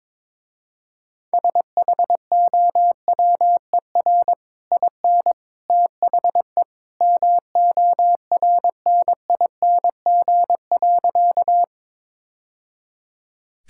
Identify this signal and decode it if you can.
Morse code, heard as a keyed tone, about 22 wpm, 700 Hz. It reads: SHOWER IN THE MORNING.
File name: practice / 22wpm / Easy Sentences with Hints - 22wpm